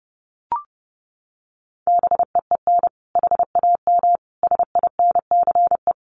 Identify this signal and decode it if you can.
Morse code, heard as a keyed tone, about 30 wpm, 700 Hz. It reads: BEEN HAM SINCE
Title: Common QSO Phrases 30wpm